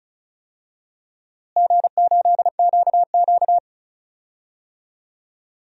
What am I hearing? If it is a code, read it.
Morse code, heard as a keyed tone, about 36 wpm, 700 Hz. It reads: G8QQ
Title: DXCC-100 Call Signs - 4 Characters - Repeated 2x - 35wpm